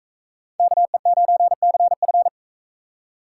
Morse code, heard as a keyed tone, about 44 wpm, 700 Hz. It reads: KE9CF